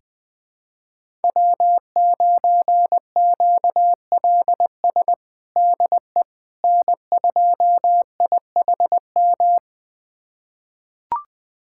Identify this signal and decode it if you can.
Morse code, heard as a keyed tone, about 21 wpm, 700 Hz. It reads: W9QLS DE N2IHM
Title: USA Callsign DE Callsign - 5 Characters - 20wpm